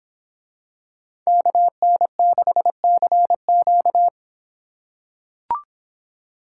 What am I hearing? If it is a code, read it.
Morse code, heard as a keyed tone, about 25 wpm, 700 Hz. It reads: KN6CQ